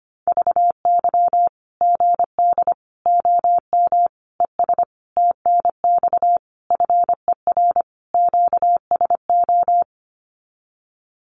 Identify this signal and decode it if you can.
Morse code, heard as a keyed tone, about 25 wpm, 700 Hz. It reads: VY GD OM ES TNX FER QSO
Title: Common QSO Phrases 25wpm